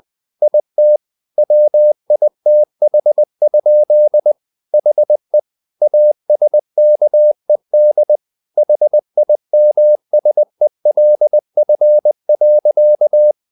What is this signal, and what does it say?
Morse code, heard as a keyed tone, about 20 wpm, 600 Hz. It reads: IT WITH? HE ASKED HIMSELF.
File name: the old man and the sea part 2 20wpm